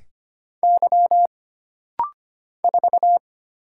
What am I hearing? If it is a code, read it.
Morse code, heard as a keyed tone, about 25 wpm, 700 Hz. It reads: Y 4
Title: Single Letter-Number - Rapid-Fire 25wpm